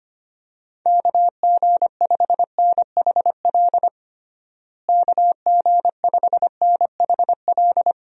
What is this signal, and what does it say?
Morse code, heard as a keyed tone, about 25 wpm, 700 Hz. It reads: KG5NHL KG5NHL
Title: USA Call Signs - 6 Characters - Repeated 2x - 25wpm